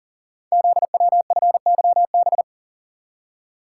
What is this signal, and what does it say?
Morse code, heard as a keyed tone, about 38 wpm, 700 Hz. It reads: ZWFYB
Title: Code Groups - 5 Letters - 40wpm